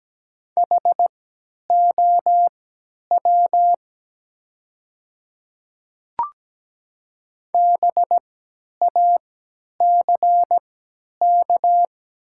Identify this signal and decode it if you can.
Morse code, heard as a keyed tone, about 17 wpm, spaced as for 10 wpm, 700 Hz. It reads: HOW BACK
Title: 3x Character Spacing - Top 300 Words - 17wpm